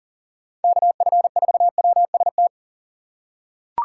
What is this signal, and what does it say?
Morse code, heard as a keyed tone, about 37 wpm, 700 Hz. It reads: KF4WST
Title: USA Call Signs - 3x Character Spacing - 40wpm